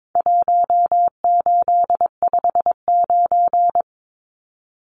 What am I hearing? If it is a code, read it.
Morse code, heard as a keyed tone, about 22 wpm, 700 Hz. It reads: 1859
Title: Numbers - 4-Digit 22wpm